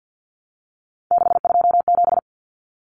Morse code, heard as a keyed tone, about 50 wpm, 700 Hz. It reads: 627
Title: Numbers - 3-Digit 50wpm